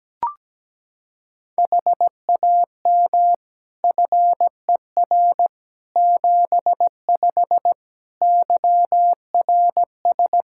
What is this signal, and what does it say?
Morse code, heard as a keyed tone, about 17 wpm, 700 Hz. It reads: HAM FER 75 YRS
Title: K1USN SST QSO Practice 17wpm